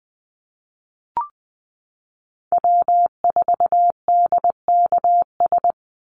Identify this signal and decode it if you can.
Morse code, heard as a keyed tone, about 20 wpm, 700 Hz. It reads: W4DKS
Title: Call Signs Repeated 3x - 20wpm